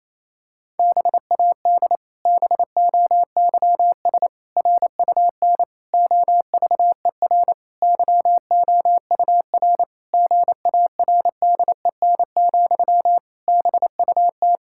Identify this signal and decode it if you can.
Morse code, heard as a keyed tone, about 28 wpm, 700 Hz. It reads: BAD BOYS RUN OVER YOUR GARDEN, BUT
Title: Pangram Challenge v2 - 28wpm